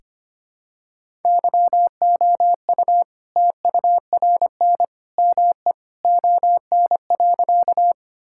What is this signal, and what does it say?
Morse code, heard as a keyed tone, about 25 wpm, 700 Hz. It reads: YOU TURN ME ON.
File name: practice / 25wpm / All-Age Appropriate Jokes - 25wpm